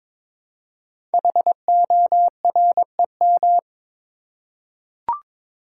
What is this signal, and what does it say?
Morse code, heard as a keyed tone, about 22 wpm, 700 Hz. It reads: HOREM
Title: Code Groups - 5 Letters - 22wpm